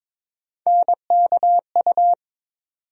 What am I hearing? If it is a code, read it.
Morse code, heard as a keyed tone, about 21 wpm, 700 Hz. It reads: NKU
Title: USA Call Sign Postfixes - 3 Letters 22wpm